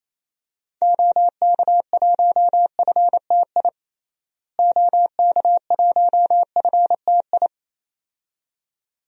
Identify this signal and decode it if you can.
Morse code, heard as a keyed tone, about 28 wpm, 700 Hz. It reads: OK1FTI OK1FTI
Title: DXCC-100 Call Signs - 6 Characters - Repeated 2x - 28wpm